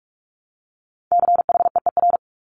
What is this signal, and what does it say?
Morse code, heard as a keyed tone, about 46 wpm, 690 Hz. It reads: CHEER
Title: Multi-letter Phonemes Repeated 3x 45wpm